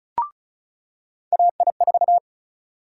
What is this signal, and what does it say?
Morse code, heard as a keyed tone, about 34 wpm, 700 Hz. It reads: AI4
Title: USA Call Sign Prefixes - 35wpm